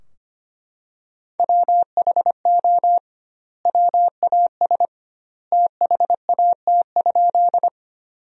Morse code, heard as a keyed tone, about 25 wpm, 700 Hz. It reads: WHO WAS THAT?
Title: 2x Word-Spacing - Sentences from Top 100 Words - 25wpm